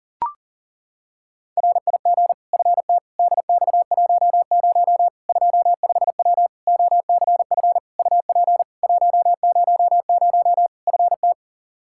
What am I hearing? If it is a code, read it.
Morse code, heard as a keyed tone, about 40 wpm, 700 Hz. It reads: RIG FT DX10 25W OCF UP 100 FT
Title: Practice QSO - Exchange 2 - Info - 40wpm